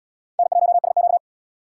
Morse code, heard as a keyed tone, about 71 wpm, 700 Hz. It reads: N1SQ